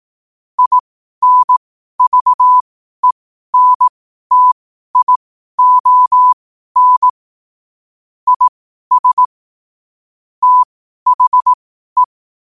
Morse code, heard as a keyed tone, about 18 wpm, spaced as for 12 wpm, 1000 Hz. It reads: INVENTION IS THE
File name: qotdfiles / 12wpm